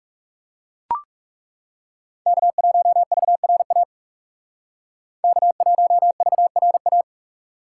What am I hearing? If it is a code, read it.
Morse code, heard as a keyed tone, about 40 wpm, 700 Hz. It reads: K1VRA K1VRA